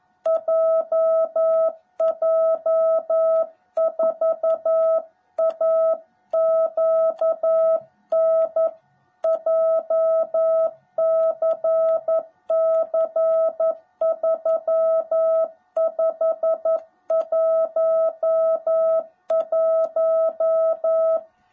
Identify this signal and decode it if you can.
Morse code, heard as a keyed tone, about 11 wpm, 660 Hz. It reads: JJ4AQN JCC3511
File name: jh8cbh / 2021cw / cw18b